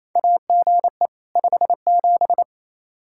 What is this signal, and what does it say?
Morse code, heard as a keyed tone, about 28 wpm, 700 Hz. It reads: AGE 57
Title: Practice QSO - Exchange 3 - More Info - 28wpm